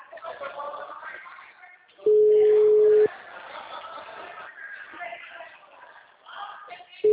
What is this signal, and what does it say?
Ringback tone (Europe, 425 Hz).